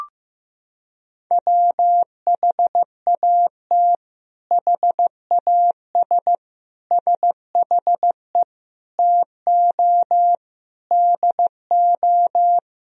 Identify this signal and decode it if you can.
Morse code, heard as a keyed tone, about 15 wpm, 700 Hz. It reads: WHAT HAS SHE TO DO
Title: Sentences from Top 100 Words and Abbreviations 15wpm